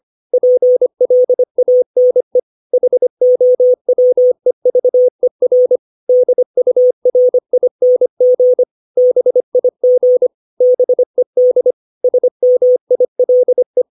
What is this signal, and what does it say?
Morse code, heard as a keyed tone, about 25 wpm, 500 Hz. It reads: PLANE HOWEVER DURING BIG BED SMILE
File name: 632 words / 632 Words 25WPM